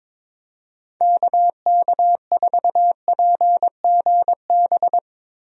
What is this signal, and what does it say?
Morse code, heard as a keyed tone, about 22 wpm, 700 Hz. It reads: KK4PGB
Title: USA Call Signs - 6 Characters - Repeated 2x - 22wpm